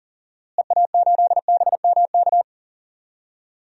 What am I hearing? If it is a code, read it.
Morse code, heard as a keyed tone, about 40 wpm, 700 Hz. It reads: EA8BMK
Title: DXCC-100 Call Signs - 6 Characters - Repeated 2x - 40wpm